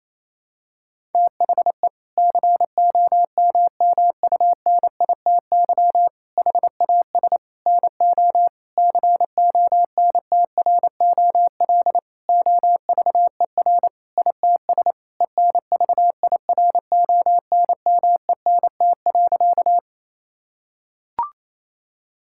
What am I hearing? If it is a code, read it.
Morse code, heard as a keyed tone, about 28 wpm, 700 Hz. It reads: THE COMMUNITY HAS NO CONTROL OVER ITS ENVIRONMENT.
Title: Sentences from Top 1000 Words - Review Part 5 - 28wpm